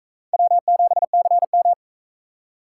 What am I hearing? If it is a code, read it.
Morse code, heard as a keyed tone, about 44 wpm, 700 Hz. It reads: W7CM